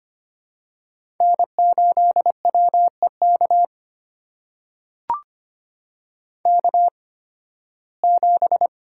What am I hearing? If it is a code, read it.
Morse code, heard as a keyed tone, about 25 wpm, spaced as for 16 wpm, 700 Hz. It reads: N8WEK K7
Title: USA Call Signs - 5 Characters - 8x Character Spacing - 25wpm